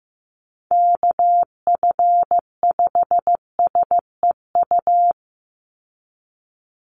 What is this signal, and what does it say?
Morse code, heard as a keyed tone, about 15 wpm, 700 Hz. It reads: KF5SEU
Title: USA Call Signs 15wpm